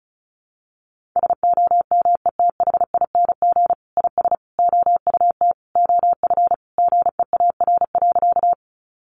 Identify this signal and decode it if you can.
Morse code, heard as a keyed tone, about 34 wpm, 700 Hz. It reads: SOMETHING IS OUT OF GEAR.
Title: Sentences from Top 1400 Words 35wpm